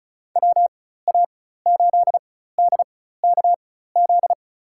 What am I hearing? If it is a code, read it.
Morse code, heard as a keyed tone, about 36 wpm, spaced as for 16 wpm, 700 Hz. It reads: WA8DKZ
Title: USA Call Signs - 6 Characters - 4x Character Spacing - 35wpm